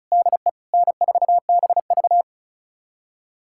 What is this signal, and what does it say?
Morse code, heard as a keyed tone, about 36 wpm, 700 Hz. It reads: DE N4BV
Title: USA Callsign DE Callsign - 4 Characters - 35wpm